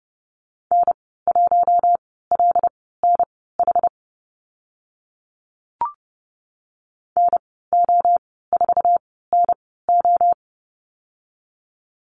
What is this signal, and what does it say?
Morse code, heard as a keyed tone, about 30 wpm, spaced as for 17 wpm, 700 Hz. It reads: N1LNH NO4NO
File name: Lesson 14g - Number 2 - Call Signs 30wpm